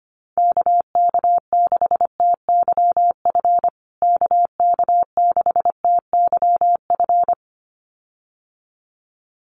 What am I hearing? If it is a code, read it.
Morse code, heard as a keyed tone, about 25 wpm, 700 Hz. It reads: KK6TYF KK6TYF
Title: USA Call Signs Repeated 2x 25wpm